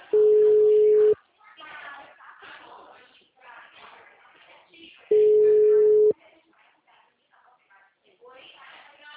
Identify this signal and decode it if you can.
Ringback tone (Europe, 425 Hz), with speech.